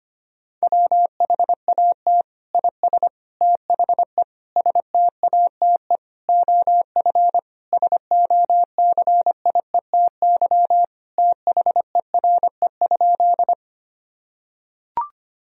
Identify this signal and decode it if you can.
Morse code, heard as a keyed tone, about 25 wpm, 700 Hz. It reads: WHAT IS THE STATE OF SOCIETY THERE?
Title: Sentences from Top 1000 Words - Review Part 4 - 25wpm